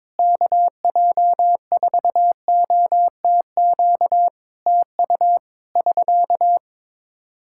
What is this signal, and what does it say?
Morse code, heard as a keyed tone, about 21 wpm, 700 Hz. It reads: KJ4OTQ TU <SK>
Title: Practice QSO - Exchange 4 - Ending - 22wpm